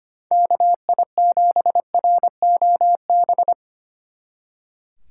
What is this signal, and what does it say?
Morse code, heard as a keyed tone, about 24 wpm, 700 Hz. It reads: KI7ROB
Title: Speed Racing - USA 2x3 Call Signs 25wpm